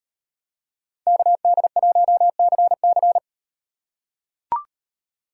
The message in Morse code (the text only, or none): KD1CC